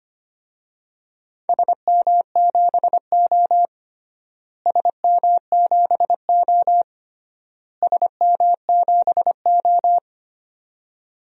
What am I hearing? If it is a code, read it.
Morse code, heard as a keyed tone, about 25 wpm, 700 Hz. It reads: SM7O SM7O SM7O
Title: DXCC-100 Call Signs - 4 Characters - Repeated 3x - 25wpm